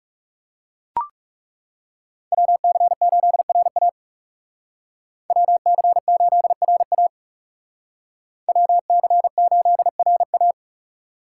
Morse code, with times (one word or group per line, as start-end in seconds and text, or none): 2.3-3.9 WC8RA
5.3-7.1 WC8RA
8.5-10.5 WC8RA